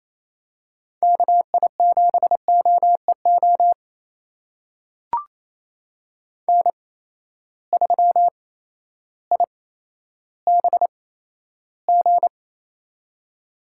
Morse code, heard as a keyed tone, about 28 wpm, spaced as for 13 wpm, 700 Hz. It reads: KI7OEO N3IBG
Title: USA Call Signs - 8x Character Spacing - 28wpm